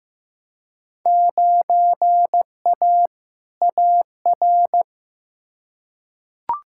Morse code, heard as a keyed tone, about 16 wpm, 700 Hz. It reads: 9A AR
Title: Field Day Exchanges Encore - 15wpm